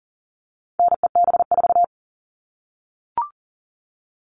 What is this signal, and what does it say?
Morse code, heard as a keyed tone, about 40 wpm, 700 Hz. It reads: NEB4
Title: All Cut Numbers - 4-Digit 40wpm